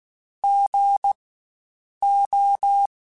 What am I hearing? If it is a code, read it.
Morse code, heard as a keyed tone, about 16 wpm, spaced as for 7 wpm, 800 Hz. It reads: GO
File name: Spaced Alphas / Spaced Alphas mp3s / Gs sp @15